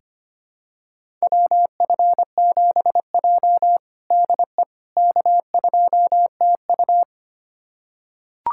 Morse code, heard as a keyed tone, about 25 wpm, 700 Hz. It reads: WF7J DE K2TU
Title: USA Callsign DE Callsign - 4 Characters - 25wpm